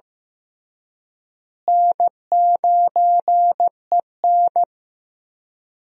Morse code, heard as a keyed tone, about 15 wpm, 700 Hz. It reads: N9EN
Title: USA Call Signs - 4 Characters - Repeated 3x - 15wpm